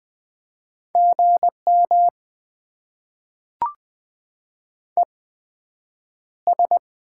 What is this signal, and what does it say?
Morse code, heard as a keyed tone, about 20 wpm, spaced as for 9 wpm, 700 Hz. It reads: GM ES
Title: Top 20 QSO Elements - 8x Character Spacing - 20wpm